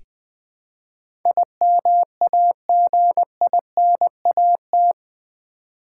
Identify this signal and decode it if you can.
Morse code, heard as a keyed tone, about 20 wpm, 700 Hz. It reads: IMAGINAT
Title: Build A Word - 4 Letters At A Time - 11-Letter Words - 20wpm